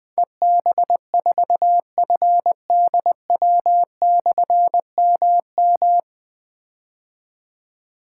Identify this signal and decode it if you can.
Morse code, heard as a keyed tone, about 20 wpm, 700 Hz. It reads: EB4FDW/MM
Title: Difficult Call Signs 20wpm